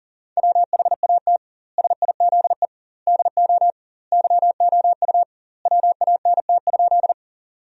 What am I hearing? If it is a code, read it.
Morse code, heard as a keyed tone, about 40 wpm, 700 Hz. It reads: WHAT SIZE DO YOU WANT?
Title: 2x Word-Spacing - Sentences from Top 400 Words - 40wpm